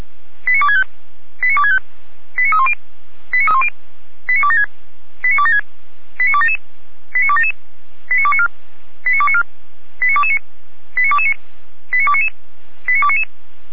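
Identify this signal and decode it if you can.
Five-tone selective call (ZVEI), heard as quick runs of five tones, tones of about 80 ms. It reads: 89265 89265 89310 89276 89276 89277 89277 89294 89294 89229 89229 89299 89299
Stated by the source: ZVEI five-tone selective call received by SDR on channel 455.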